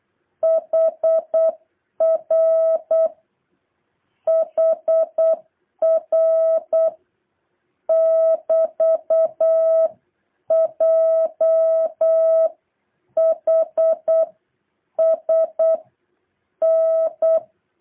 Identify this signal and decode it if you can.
Morse code, heard as a keyed tone, about 8 wpm, 650 Hz. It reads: HR HR <BT>JHS N